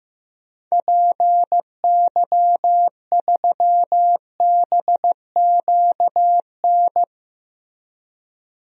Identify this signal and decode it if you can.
Morse code, heard as a keyed tone, about 16 wpm, 700 Hz. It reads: PY3BQN
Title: DXCC-100 Call Signs - 6 Characters - Repeated 2x - 15wpm